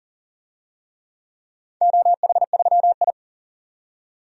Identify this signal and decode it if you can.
Morse code, heard as a keyed tone, about 40 wpm, 700 Hz. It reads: OH3I
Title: DXCC-100 Call Signs - 4 Characters - 40wpm